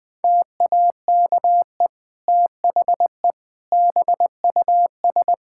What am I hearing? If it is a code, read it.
Morse code, heard as a keyed tone, about 20 wpm, 700 Hz. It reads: TAKE THE BUS
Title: Sentences from Top 1000 Words - Review Part 6 - 20wpm